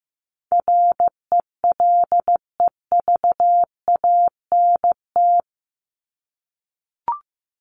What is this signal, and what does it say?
Morse code, heard as a keyed tone, about 15 wpm, 700 Hz. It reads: RELEVANT
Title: Top 1001-1250 Words 15wpm